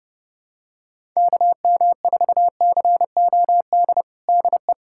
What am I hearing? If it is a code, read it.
Morse code, heard as a keyed tone, about 30 wpm, 700 Hz. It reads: KM4COD DE
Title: USA Callsign DE Callsign - 6 Characters - 30wpm